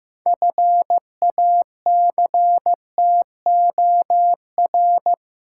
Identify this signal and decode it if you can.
Morse code, heard as a keyed tone, about 15 wpm, 700 Hz. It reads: FACTOR
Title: Sentences from Top 1300 Words - New Word Drill Alt - 15wpm